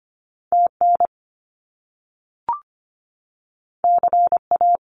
Morse code, heard as a keyed tone, about 25 wpm, 700 Hz. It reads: TN CA
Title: US States 25wpm